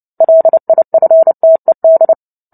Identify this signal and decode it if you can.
Morse code, heard as a keyed tone, about 30 wpm, 650 Hz. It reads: LIFTED
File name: Book Two / 03-Chapter 3 - 29 WPM